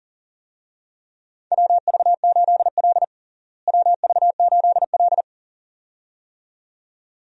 Morse code, heard as a keyed tone, about 40 wpm, 700 Hz. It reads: WV8L WV8L
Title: USA Call Signs - 4 Characters - Repeated 2x - 40wpm